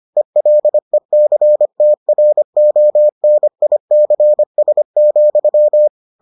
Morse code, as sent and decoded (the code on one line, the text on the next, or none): . .-.. -.-. - .-. --- -. .. -.-. ... --..--
ELECTRONICS,